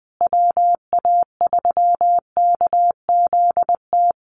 Morse code, heard as a keyed tone, about 19 wpm, 700 Hz. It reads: WA3KZT
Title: USA Call Signs Repeated 3x 20wpm